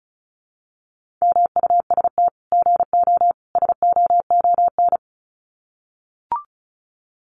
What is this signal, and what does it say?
Morse code, heard as a keyed tone, about 34 wpm, 700 Hz. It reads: MUST GO SOON
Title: Common QSO Phrases 35wpm